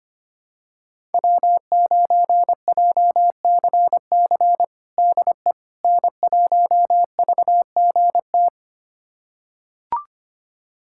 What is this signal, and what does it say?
Morse code, heard as a keyed tone, about 25 wpm, 700 Hz. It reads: W9JCC DE N1VGT
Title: USA Callsign DE Callsign - 5 Characters - 25wpm